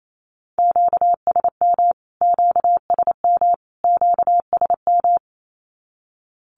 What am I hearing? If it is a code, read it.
Morse code, heard as a keyed tone, about 28 wpm, 700 Hz. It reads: QSM QSM QSM
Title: Q-Codes Repeated 3x 28wpm